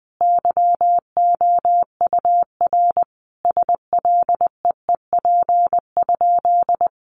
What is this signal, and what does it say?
Morse code, heard as a keyed tone, about 20 wpm, 700 Hz. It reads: YOUR SLEEP?
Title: Sentences from Top 500 Words 20wpm